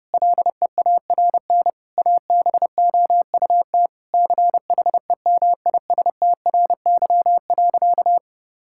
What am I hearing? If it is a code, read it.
Morse code, heard as a keyed tone, about 32 wpm, 700 Hz. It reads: LEARN ABOUT CHEMISTRY.
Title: Sentences from Top 1000 Words - Review Part 6 - 30wpm